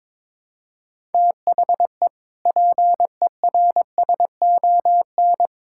Morse code, heard as a keyed tone, about 22 wpm, 700 Hz. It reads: THE PERSON
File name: practice / 22wpm / Sentences with 8-Word Phrases - 22wpm